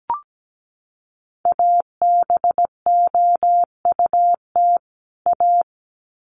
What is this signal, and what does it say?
Morse code, heard as a keyed tone, about 17 wpm, 700 Hz. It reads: ABOUT A
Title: Sets of 2 Words 17wpm